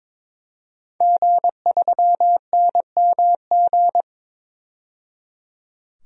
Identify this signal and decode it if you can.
Morse code, heard as a keyed tone, about 22 wpm, 700 Hz. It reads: G3NMG
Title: DXCC-100 Call Signs - 5 Characters - 22wpm